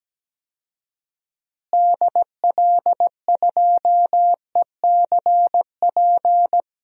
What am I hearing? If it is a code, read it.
Morse code, heard as a keyed tone, about 17 wpm, 700 Hz. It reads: DL2ECP
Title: DXCC-100 Call Signs - 6 Characters - Repeated 3x - 17wpm